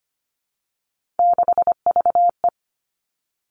Morse code, heard as a keyed tone, about 25 wpm, 700 Hz. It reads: 6VE
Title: Three Challenging Letter-Number Combinations 25wpm